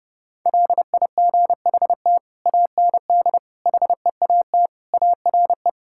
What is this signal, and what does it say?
Morse code, heard as a keyed tone, about 30 wpm, 700 Hz. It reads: LIGHT AND HEAT ARE
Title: Sentences from Top 1000 Words - Review Part 5 - 30wpm